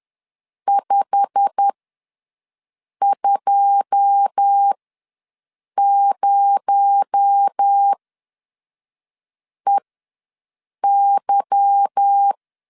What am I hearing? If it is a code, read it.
Morse code, heard as a keyed tone, about 10 wpm, spaced as for 5 wpm, 800 Hz. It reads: 520EY
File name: LEZIONI / 9 2 ws 10wpm 4130